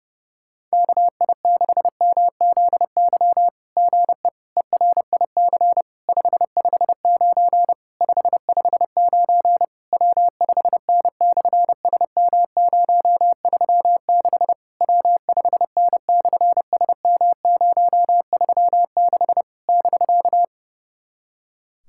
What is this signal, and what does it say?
Morse code, heard as a keyed tone, about 31 wpm, 700 Hz. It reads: KI6MZY GE ERIC 559 559 W5N/SM036 W5N/SM036 <BK>
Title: SOTA QSO Practice 30wpm